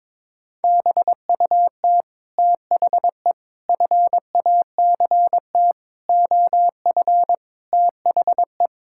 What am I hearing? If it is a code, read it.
Morse code, heard as a keyed tone, about 22 wpm, 700 Hz. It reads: BUT THE FACT OF THE
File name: Sentences with 9-Word Phrases - 22wpm